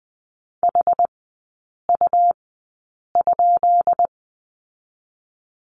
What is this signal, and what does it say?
Morse code, heard as a keyed tone, about 20 wpm, spaced as for 8 wpm, 700 Hz. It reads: HU?
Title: 205 Lesson 33e - Letter j - Three Characters 20wpm